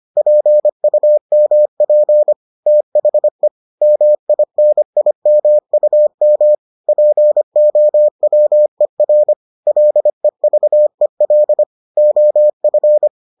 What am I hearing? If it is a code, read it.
Morse code, heard as a keyed tone, about 25 wpm, 600 Hz. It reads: PUMP THE MINIMUM POWER LEVEL OF